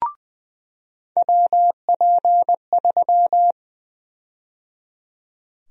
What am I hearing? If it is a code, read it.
Morse code, heard as a keyed tone, about 20 wpm, 700 Hz. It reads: WP3
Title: USA Call Sign Prefixes - 20wpm